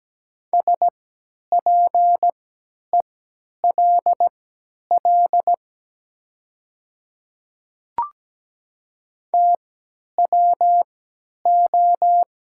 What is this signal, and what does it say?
Morse code, heard as a keyed tone, about 17 wpm, spaced as for 10 wpm, 700 Hz. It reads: SPELL TWO
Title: 3x Character Spacing - Top 200 Words - 17wpm